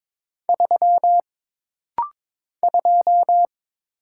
Morse code, heard as a keyed tone, about 22 wpm, 700 Hz. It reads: 3 2
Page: This rapid-fire timing is geared to push your ICR (Instant Character Recognition) skills.